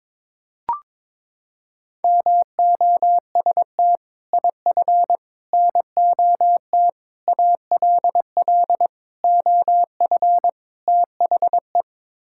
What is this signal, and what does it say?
Morse code, heard as a keyed tone, about 22 wpm, 700 Hz. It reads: MOST IF NOT ALL OF THE
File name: practice / 22wpm / Sets of 6 Words - 22wpm